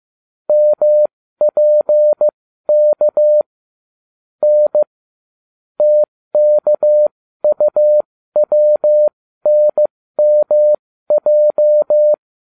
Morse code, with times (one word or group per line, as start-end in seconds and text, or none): 0.5-3.4 MPK
4.4-4.8 N
5.8-12.1 TKUWNMJ